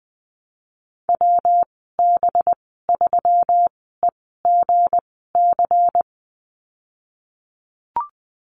Morse code, heard as a keyed tone, about 20 wpm, spaced as for 15 wpm, 700 Hz. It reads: WB3EGC
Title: 200 Lesson 32g - Letter k - Call Signs 20wpm